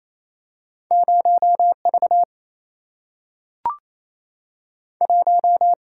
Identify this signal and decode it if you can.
Morse code, heard as a keyed tone, about 28 wpm, 700 Hz. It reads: 0V 1
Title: Single-Letter Single-Number 28wpm